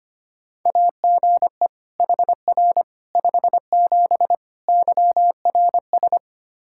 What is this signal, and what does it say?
Morse code, heard as a keyed tone, about 25 wpm, 700 Hz. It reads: AGE HR 57 YRS
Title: Practice QSO - Exchange 3 - More Info - 25wpm